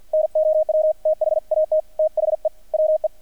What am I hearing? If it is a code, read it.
Morse code, heard as a keyed tone, about 50 wpm, 620 Hz. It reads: NOW THAT THE WE
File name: archived / CRSnet